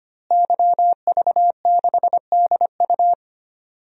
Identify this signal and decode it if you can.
Morse code, heard as a keyed tone, about 25 wpm, 700 Hz. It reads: YV6DU
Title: DXCC-100 Call Signs - 5 Characters - Repeated 3x - 25wpm